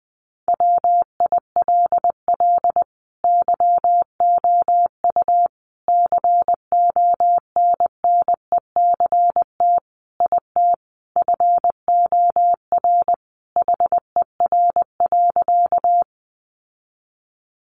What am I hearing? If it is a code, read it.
Morse code, heard as a keyed tone, about 20 wpm, 700 Hz. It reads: WILL YOU CONNECT IT FOR HER.
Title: Sentences from Top 1500 Words 20wpm